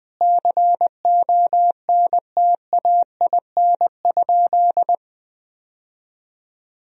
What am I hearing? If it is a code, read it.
Morse code, heard as a keyed tone, about 19 wpm, 700 Hz. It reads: CONTAIN?
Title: Sentences from Top 1000 Words - Review Part 2 - 20wpm